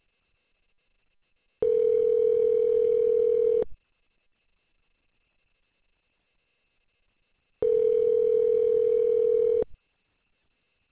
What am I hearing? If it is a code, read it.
Ringback tone (North America), 440 + 480 Hz.